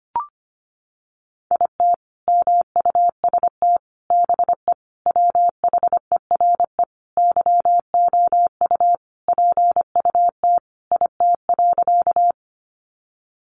Morse code, heard as a keyed tone, about 25 wpm, 700 Hz. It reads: IT MUST BE WHERE YOU PUT IT.
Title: Sentences from Top 200 Words 25wpm